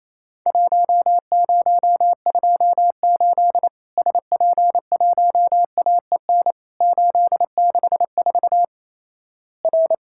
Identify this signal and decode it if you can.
Morse code, heard as a keyed tone, about 28 wpm, 700 Hz. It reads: 1028 SP1AEN 864 R
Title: Worked All Europe DX Contest - QTC Exchange - 28wpm